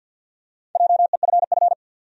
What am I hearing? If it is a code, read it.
Morse code, heard as a keyed tone, about 50 wpm, 700 Hz. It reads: JEFF
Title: USA Common Names - Max 4 Letters - 50wpm